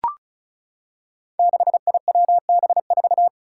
Morse code, heard as a keyed tone, about 36 wpm, 700 Hz. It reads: 6IWB4
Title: Five Challenging Letter-Number Combinations 35wpm